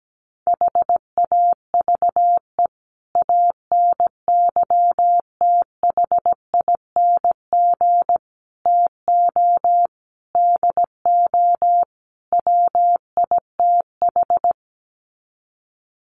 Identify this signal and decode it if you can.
Morse code, heard as a keyed tone, about 17 wpm, 700 Hz. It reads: HAVE ANYTHING TO DO WITH
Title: Sets of 5 Words 17wpm